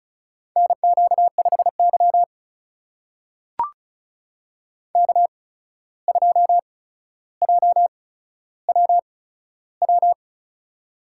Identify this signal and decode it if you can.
Morse code, heard as a keyed tone, about 36 wpm, spaced as for 14 wpm, 700 Hz. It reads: NQ5Y K2JWW
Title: USA Call Signs - 8x Character Spacing - 35wpm